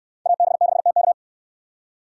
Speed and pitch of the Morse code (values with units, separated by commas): 68 wpm, 700 Hz